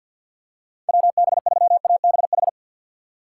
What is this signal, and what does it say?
Morse code, heard as a keyed tone, about 53 wpm, 700 Hz. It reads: WB3ABH
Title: USA Call Signs - 6 Characters - 6x Character Spacing - 50wpm